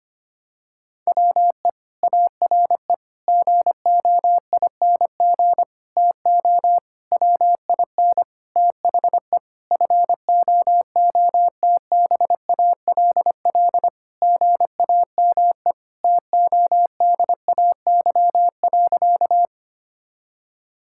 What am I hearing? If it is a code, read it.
Morse code, heard as a keyed tone, about 25 wpm, 700 Hz. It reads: WE ARE GOING TO WIN THE FOOTBALL GAME TODAY.
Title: Sentences from Top 1600 Words 25wpm